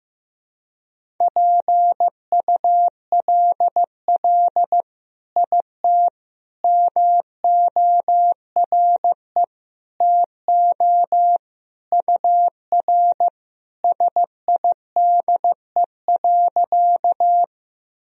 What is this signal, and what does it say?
Morse code, heard as a keyed tone, about 16 wpm, 700 Hz. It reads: PULL IT MORE TO UR SIDE.
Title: Sentences from Top 400 Words and Abbreviations 15wpm